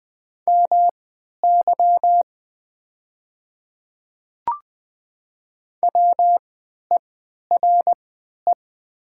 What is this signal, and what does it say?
Morse code, heard as a keyed tone, about 20 wpm, spaced as for 11 wpm, 700 Hz. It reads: MY WERE
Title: 3x Character Spacing - Top 100 Words - 20wpm